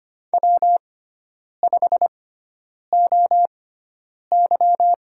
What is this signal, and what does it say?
Morse code, heard as a keyed tone, about 25 wpm, spaced as for 9 wpm, 700 Hz. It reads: W5OY
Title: USA Call Signs - 6x Character Spacing - 25wpm